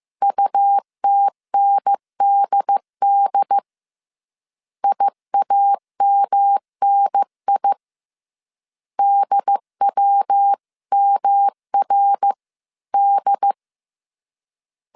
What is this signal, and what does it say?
Morse code, heard as a keyed tone, about 15 wpm, 800 Hz. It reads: UTNDD IAMNI DWMRD